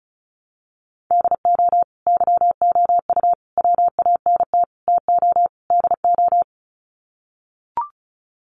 Morse code, heard as a keyed tone, about 34 wpm, 700 Hz. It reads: DO YOU WANT TO DO
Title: Sets of 5 Words Encore 35wpm